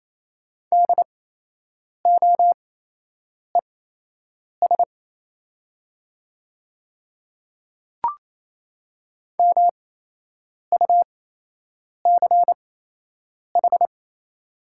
Morse code, heard as a keyed tone, about 28 wpm, spaced as for 8 wpm, 700 Hz. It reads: DOES MUCH
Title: 8x Character Spacing - Top 200 Words - 28wpm